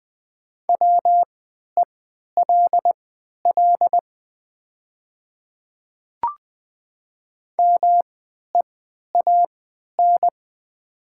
Morse code, heard as a keyed tone, about 20 wpm, spaced as for 11 wpm, 700 Hz. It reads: WELL MEAN
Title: 3x Character Spacing - Top 500 Words - 20wpm